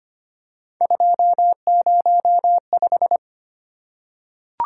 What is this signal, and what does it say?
Morse code, heard as a keyed tone, about 25 wpm, 700 Hz. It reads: 205
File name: Three Letter-Number Combinations - 25wpm